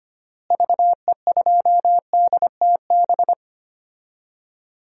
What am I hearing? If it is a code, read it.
Morse code, heard as a keyed tone, about 25 wpm, 700 Hz. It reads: VE2DTB
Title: DXCC-100 Call Signs - 6 Characters - Repeated 3x - 25wpm